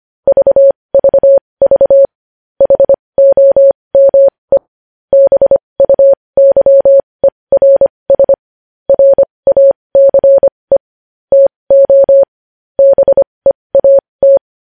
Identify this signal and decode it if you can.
Morse code, heard as a keyed tone, about 25 wpm, spaced as for 20 wpm, 550 Hz. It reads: VVV HOME BUYERS RACE TO BEAT
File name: podcast / News headlines-1x-25x20